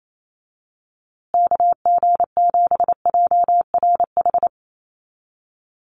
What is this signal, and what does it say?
Morse code, heard as a keyed tone, about 28 wpm, 700 Hz. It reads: KG7JRH
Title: USA Call Signs Repeated 2x 28wpm